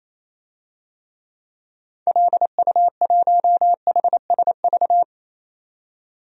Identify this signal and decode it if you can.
Morse code, heard as a keyed tone, about 28 wpm, 700 Hz. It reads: LU1HSV
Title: DXCC-100 Call Signs - 6 Characters - 28wpm